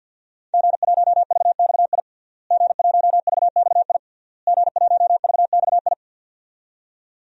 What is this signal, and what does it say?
Morse code, heard as a keyed tone, about 52 wpm, 700 Hz. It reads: G1VXI G1VXI G1VXI
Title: DXCC-100 Call Signs - 5 Characters - Repeated 3x - 50wpm